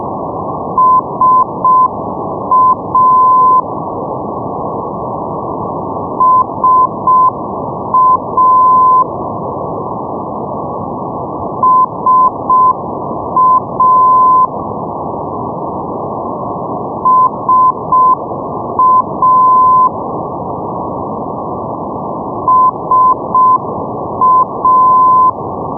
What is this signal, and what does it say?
Morse code, heard as a keyed tone, about 6 wpm, 1020 Hz, with noise. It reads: SA SA SA SA SA
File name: europe / pol / pol-531 ndb darlowo sa 474.5 (rug) 2008 rb